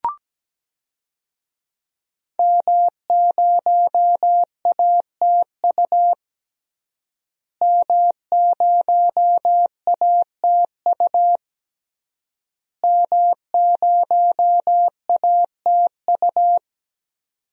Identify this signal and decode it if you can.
Morse code, heard as a keyed tone, about 17 wpm, 700 Hz. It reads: M0ATU M0ATU M0ATU
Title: DXCC-100 Call Signs - 5 Characters - Repeated 3x - 17wpm